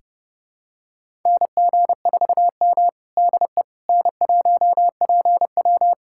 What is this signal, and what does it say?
Morse code, heard as a keyed tone, about 30 wpm, 700 Hz. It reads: NG4M DE N1PW
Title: USA Callsign DE Callsign - 4 Characters - 30wpm